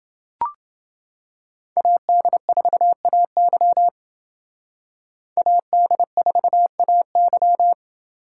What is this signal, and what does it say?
Morse code, heard as a keyed tone, about 29 wpm, 700 Hz. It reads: AD4AY AD4AY